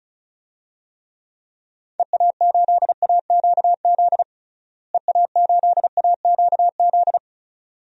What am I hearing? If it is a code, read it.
Morse code, heard as a keyed tone, about 34 wpm, 700 Hz. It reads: EA8AQZ EA8AQZ
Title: DXCC-100 Call Signs - 6 Characters - Repeated 2x - 35wpm